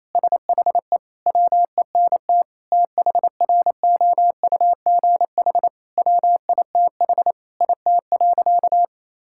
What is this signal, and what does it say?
Morse code, heard as a keyed tone, about 28 wpm, 700 Hz. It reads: SHE WENT THROUGH WITH IT.
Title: Sentences from Top 1000 Words - Review Part 6 - 28wpm